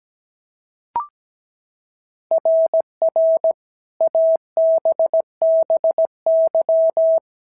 Morse code, heard as a keyed tone, about 17 wpm, 650 Hz. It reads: RR ABBY
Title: SOTA QSO Practice 17wpm